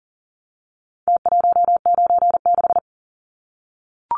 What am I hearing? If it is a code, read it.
Morse code, heard as a keyed tone, about 40 wpm, 700 Hz. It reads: T196
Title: Numbers - Contest Style - 4-Digit 40wpm